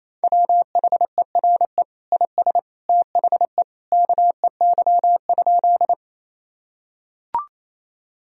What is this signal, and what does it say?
Morse code, heard as a keyed tone, about 27 wpm, 700 Hz. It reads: WHERE IS THE KEY?
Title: Sentences from Top 1000 Words - Review Part 4 - 28wpm